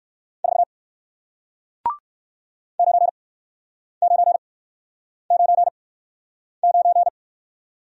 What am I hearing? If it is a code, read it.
Morse code, heard as a keyed tone, about 59 wpm, spaced as for 9 wpm, 700 Hz. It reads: 4 9999